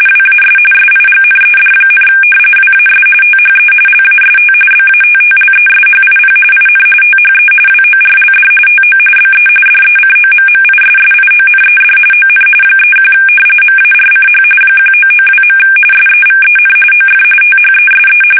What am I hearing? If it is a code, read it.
RTTY, about 45.45 baud. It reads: WAS SIGNED, WAS THE LAST BATTLESHIP IN THE WORLD TO BE DECOMMISSIONED ON 31 MARCH 1992. SEVEN OF THESE
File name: fsk850 / battleships